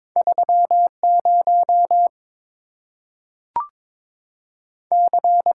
Morse code, heard as a keyed tone, about 22 wpm, 700 Hz. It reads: 3 0 C
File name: practice / 22wpm / Two Letter-Number Combinations - 22wpm